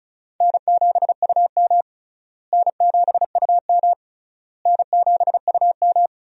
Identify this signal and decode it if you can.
Morse code, heard as a keyed tone, about 36 wpm, 700 Hz. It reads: N7UM N7UM N7UM
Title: USA Call Signs - 4 Characters - Repeated 3x - 35wpm